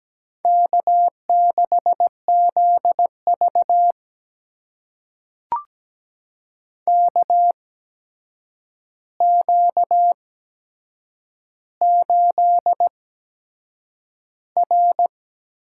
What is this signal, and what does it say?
Morse code, heard as a keyed tone, about 17 wpm, spaced as for 7 wpm, 700 Hz. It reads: K6ZV KQ8R
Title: USA Call Signs - 4 Characters - 8x Character Spacing - 17wpm